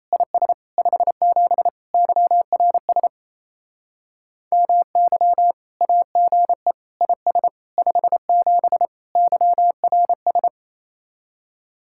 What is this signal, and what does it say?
Morse code, heard as a keyed tone, about 35 wpm, 700 Hz. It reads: IS 57 YRS MY AGE IS 57 YRS